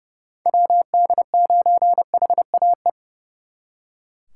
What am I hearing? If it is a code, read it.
Morse code, heard as a keyed tone, about 35 wpm, 700 Hz. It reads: WD9HAE
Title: Speed Racing - USA 2x3 Call Signs 30wpm